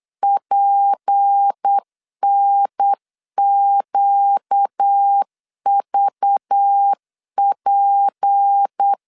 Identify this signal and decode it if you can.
Morse code, heard as a keyed tone, about 8 wpm, 800 Hz. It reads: PNQVP
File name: LEZIONI / 6 PKX0 / 6 P 8wpm 3025